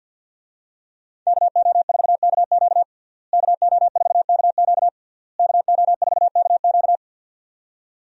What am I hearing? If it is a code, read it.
Morse code, heard as a keyed tone, about 52 wpm, 700 Hz. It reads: KO4KQ KO4KQ KO4KQ
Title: USA Call Signs - 5 Characters - Repeated 3x - 50wpm